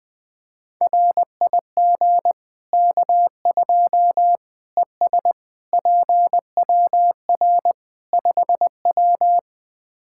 Morse code, heard as a keyed tone, about 21 wpm, 700 Hz. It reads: RIG K2 ES PWR 5W
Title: Practice QSO - Exchange 2 - Info - 20wpm